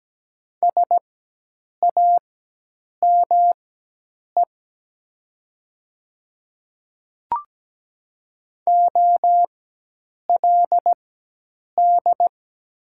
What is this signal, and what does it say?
Morse code, heard as a keyed tone, about 17 wpm, spaced as for 8 wpm, 700 Hz. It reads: SAME OLD
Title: 4x Character Spacing - Top 300 Words - 17wpm